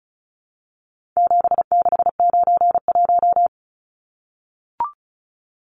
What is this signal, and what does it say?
Morse code, heard as a keyed tone, about 34 wpm, 700 Hz. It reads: 7691
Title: Numbers - 4-Digit 35wpm